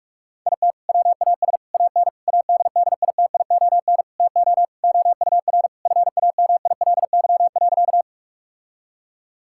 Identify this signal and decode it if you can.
Morse code, heard as a keyed tone, about 46 wpm, 700 Hz. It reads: IT WAS AN ADDITION TO OUR FAMILY.
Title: Sentences from Top 1000 Words - Review Part 6 - 45wpm